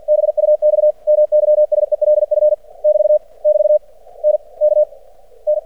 Morse code, heard as a keyed tone, about 49 wpm, 600 Hz. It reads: BUX MYSELF <BT> <BT> N K N
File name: archived / CRSnet